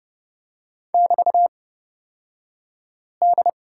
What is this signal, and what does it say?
Morse code, heard as a keyed tone, about 31 wpm, spaced as for 5 wpm, 700 Hz. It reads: <BT>D
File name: Lesson 36d - BT - Two Characters 30wpm